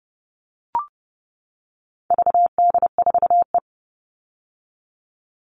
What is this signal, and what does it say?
Morse code, heard as a keyed tone, about 30 wpm, 700 Hz. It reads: VD4E
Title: All Cut Numbers - 4-Digit 30wpm